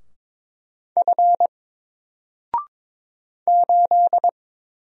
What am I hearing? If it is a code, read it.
Morse code, heard as a keyed tone, about 22 wpm, 700 Hz. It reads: F 8
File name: Single-Letter-Number-22wpm